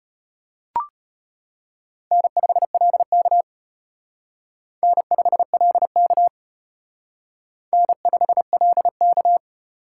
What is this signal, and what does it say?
Morse code, heard as a keyed tone, about 32 wpm, 700 Hz. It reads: N5LK N5LK N5LK